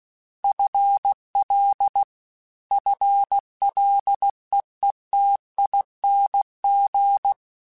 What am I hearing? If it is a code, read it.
Morse code, heard as a keyed tone, about 16 wpm, 800 Hz. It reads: FL FLEETING
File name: FL / FL mp3s / FL Prefixes 1x1 @15